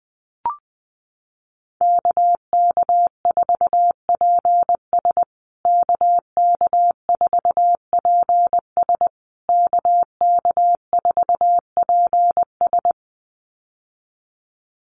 Morse code, heard as a keyed tone, about 20 wpm, 700 Hz. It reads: KK4PS KK4PS KK4PS
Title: USA Call Signs Repeated 3x 20wpm